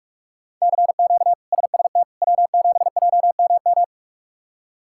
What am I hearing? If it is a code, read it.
Morse code, heard as a keyed tone, about 43 wpm, 700 Hz. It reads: CQ SST W7JMM
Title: K1USN SST QSO Practice - 45wpm